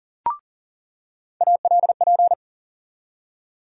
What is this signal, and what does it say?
Morse code, heard as a keyed tone, about 40 wpm, 700 Hz. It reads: ALP
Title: USA Call Sign Postfixes - 3 Letters 40wpm